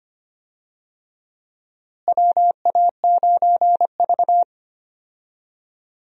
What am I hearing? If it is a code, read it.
Morse code, heard as a keyed tone, about 25 wpm, 700 Hz. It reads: WA9V
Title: USA Call Signs - 4 Characters - Repeated 2x - 25wpm